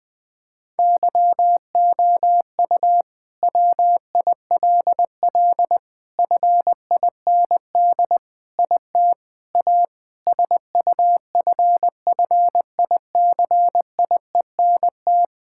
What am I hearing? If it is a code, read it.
Morse code, heard as a keyed tone, about 20 wpm, 700 Hz. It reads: YOU WILL FIND IT A SUFFICIENT
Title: Sentences from Top 1600 Words 20wpm